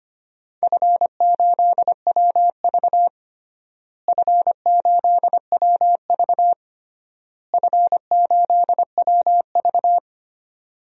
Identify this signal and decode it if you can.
Morse code, heard as a keyed tone, about 25 wpm, 700 Hz. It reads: F8WV F8WV F8WV
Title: DXCC-100 Call Signs - 4 Characters - Repeated 3x - 25wpm